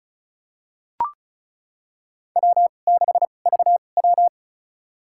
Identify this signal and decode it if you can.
Morse code, heard as a keyed tone, about 36 wpm, spaced as for 25 wpm, 700 Hz. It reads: W6VW
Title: USA Call Signs - 4 Characters - 2x Character Spacing - 35wpm